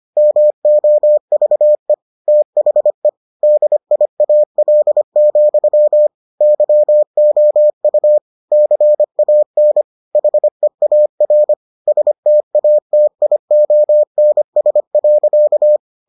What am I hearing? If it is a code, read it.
Morse code, heard as a keyed tone, about 25 wpm, 600 Hz. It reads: MOVE THE DIAL, YOU CAN HEAR STATIONS.